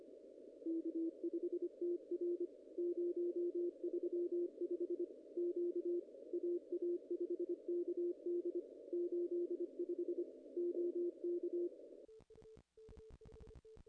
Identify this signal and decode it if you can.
Morse code, heard as a keyed tone, about 25 wpm, 350 Hz. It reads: K5TR 035 Q AA5KD 85 OK
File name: contests / ss-cw-k5tr-op-n5rz -0000